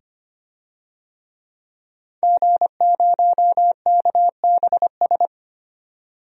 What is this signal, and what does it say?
Morse code, heard as a keyed tone, about 25 wpm, 700 Hz. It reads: G0KBS
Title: DXCC-100 Call Signs - 5 Characters - 25wpm